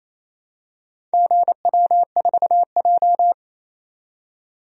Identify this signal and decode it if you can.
Morse code, heard as a keyed tone, about 28 wpm, 700 Hz. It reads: GW4J
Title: DXCC-100 Call Signs - 4 Characters - 28wpm